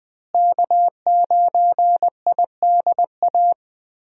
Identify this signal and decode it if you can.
Morse code, heard as a keyed tone, about 20 wpm, 700 Hz. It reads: K9IDA
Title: USA Call Signs - 5 Characters - Repeated 3x - 20wpm